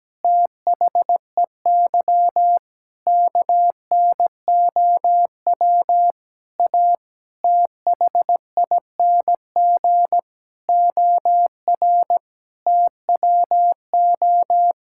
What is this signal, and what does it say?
Morse code, heard as a keyed tone, about 17 wpm, 700 Hz. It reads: THEY KNOW A THING OR TWO
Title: Sentences from Top 1000 Words - Review Part 5 - 17wpm